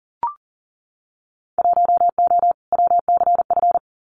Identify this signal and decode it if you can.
Morse code, heard as a keyed tone, about 41 wpm, 700 Hz. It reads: 1O WCF
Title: Winter Field Day Exchanges 40wpm